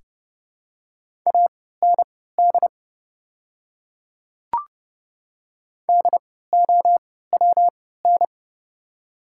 Morse code, heard as a keyed tone, about 30 wpm, spaced as for 17 wpm, 700 Hz. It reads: AND DOWN
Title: 3x Character Spacing - Top 200 Words - 30wpm